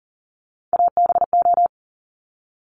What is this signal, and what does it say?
Morse code, heard as a keyed tone, about 42 wpm, 700 Hz. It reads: ABO
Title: Top Three Letter Combinations in Words 40wpm